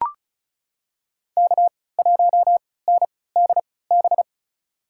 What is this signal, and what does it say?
Morse code, heard as a keyed tone, about 36 wpm, spaced as for 20 wpm, 700 Hz. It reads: K1NDB
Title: USA Call Signs - 3x Character Spacing - 35wpm